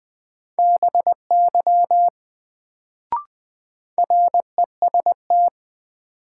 Morse code, heard as a keyed tone, about 20 wpm, 700 Hz. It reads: BY REST
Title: Sets of 2 Words - New Word Drill - 20wpm